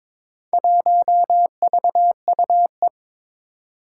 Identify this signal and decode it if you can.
Morse code, heard as a keyed tone, about 22 wpm, 700 Hz. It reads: 1VUE
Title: Four Challenging Letter-Number Combinations 22wpm